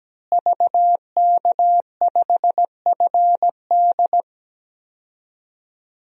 Morse code, heard as a keyed tone, about 17 wpm, 700 Hz. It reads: VK5FD
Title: DXCC-100 Call Signs - 5 Characters - Repeated 3x - 17wpm